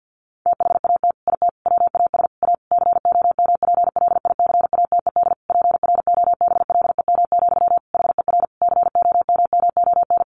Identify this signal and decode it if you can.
Morse code, heard as a keyed tone, about 50 wpm, 700 Hz. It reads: THAT IT WAS A COMPLICATED PROBLEM, HER COMMON
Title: Sentences from Top 1200 Words 50wpm